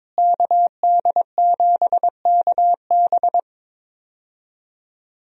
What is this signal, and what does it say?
Morse code, heard as a keyed tone, about 22 wpm, 700 Hz. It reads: KD7KB
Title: USA Call Signs - 5 Characters - Repeated 2x - 22wpm